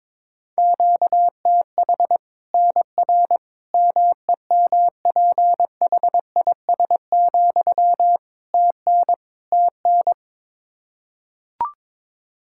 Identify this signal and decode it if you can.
Morse code, heard as a keyed tone, about 22 wpm, 700 Hz. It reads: QTH NR MEMPHIS, TN TN
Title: Practice QSO - Exchange 1 - Beginning - 22wpm